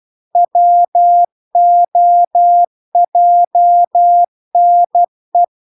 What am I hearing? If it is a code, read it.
Morse code, heard as a keyed tone, about 12 wpm, 700 Hz. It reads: WOJNE